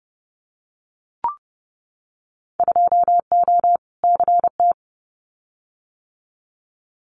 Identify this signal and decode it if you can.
Morse code, heard as a keyed tone, about 30 wpm, 700 Hz. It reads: 2O CT
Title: Winter Field Day Exchanges 30wpm